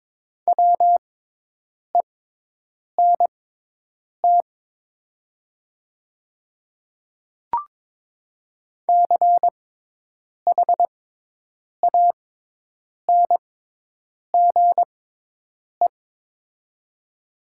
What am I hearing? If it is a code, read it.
Morse code, heard as a keyed tone, about 22 wpm, spaced as for 8 wpm, 700 Hz. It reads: WENT CHANGE
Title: 6x Character Spacing - Top 300 Words - 22wpm